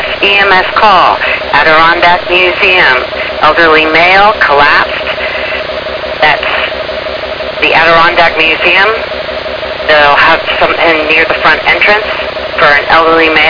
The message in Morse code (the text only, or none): none